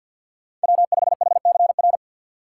Morse code, heard as a keyed tone, about 52 wpm, 700 Hz. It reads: W5HCR
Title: USA Call Signs - 5 Characters - 3x Character Spacing - 50wpm